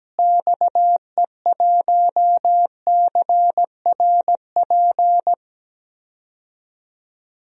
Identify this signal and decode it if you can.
Morse code, heard as a keyed tone, about 17 wpm, 700 Hz. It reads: XE1CRP
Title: DXCC-100 Call Signs - 6 Characters - Repeated 3x - 17wpm